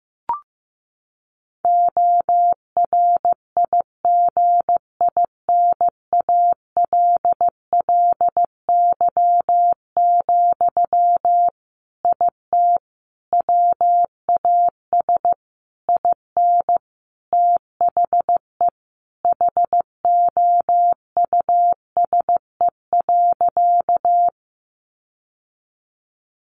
Morse code, heard as a keyed tone, about 15 wpm, 700 Hz. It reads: ORIGINALLY, IT WAS IN THE HOUSE.
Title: Sentences from Top 1500 Words 15wpm